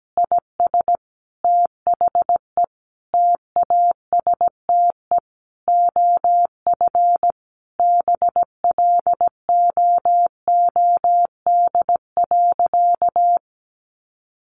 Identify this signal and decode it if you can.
Morse code, heard as a keyed tone, about 17 wpm, 700 Hz. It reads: IS THE TASTE OF BLOOD.
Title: Sentences from Top 1200 Words 17wpm